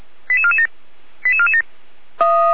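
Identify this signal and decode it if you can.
Five-tone selective call (ZVEI), heard as quick runs of five tones, tones of about 70 ms. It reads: 88447 88447
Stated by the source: Five-tone ZVEI selective call received by SDR, logged on server of Feuerwehr Orlamünde (fire brigade).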